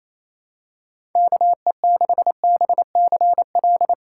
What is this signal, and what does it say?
Morse code, heard as a keyed tone, about 28 wpm, 700 Hz. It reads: KE6BCL
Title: USA Call Signs - 6 Characters - 3x Character Spacing - 28wpm